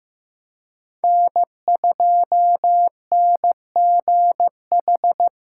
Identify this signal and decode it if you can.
Morse code, heard as a keyed tone, about 15 wpm, 700 Hz. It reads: N2NGH
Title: USA Callsign DE Callsign 15wpm